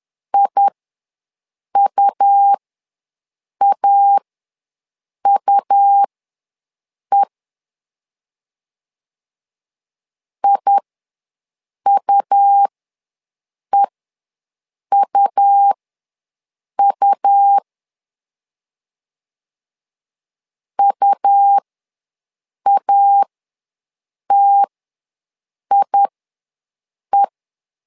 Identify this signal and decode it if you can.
Morse code, heard as a keyed tone, about 11 wpm, spaced as for 6 wpm, 800 Hz. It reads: IUAUE IUEUU UATIE
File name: LEZIONI / 2 UHDN / 2 U ws 10wpm 2017